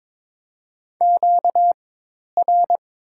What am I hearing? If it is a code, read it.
Morse code, heard as a keyed tone, about 22 wpm, spaced as for 10 wpm, 700 Hz. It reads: QR
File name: Top 90 QSO Elements - 4x Character Spacing - 22wpm